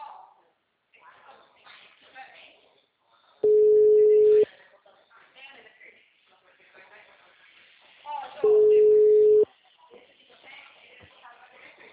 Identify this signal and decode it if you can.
Ringback tone (Europe, 425 Hz), with speech.
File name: out- -2015- - - .4